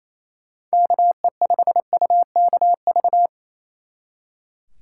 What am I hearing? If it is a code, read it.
Morse code, heard as a keyed tone, about 37 wpm, 700 Hz. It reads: KE5UKV